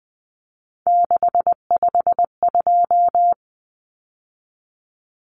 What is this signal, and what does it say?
Morse code, heard as a keyed tone, about 20 wpm, 700 Hz. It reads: 652
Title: Numbers - 3-Digit 20wpm